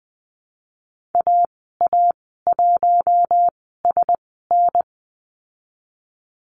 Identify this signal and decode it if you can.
Morse code, heard as a keyed tone, about 20 wpm, spaced as for 14 wpm, 700 Hz. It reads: AA1SN
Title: 055 Lesson 10g - Letter r - Call Signs 20wpm